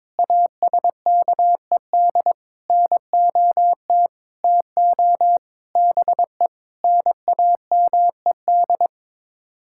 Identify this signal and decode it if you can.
Morse code, heard as a keyed tone, about 22 wpm, 700 Hz. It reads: ASKED NOT TO BE NAMED
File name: practice / 22wpm / Sets of 6 Words - 22wpm